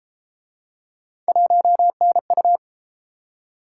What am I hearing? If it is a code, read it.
Morse code, heard as a keyed tone, about 31 wpm, 700 Hz. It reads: 1NU